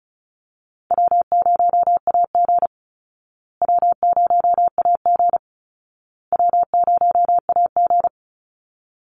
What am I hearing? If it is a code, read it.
Morse code, heard as a keyed tone, about 34 wpm, 700 Hz. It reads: W0AG W0AG W0AG
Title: Call Signs Repeated 2x - 35wpm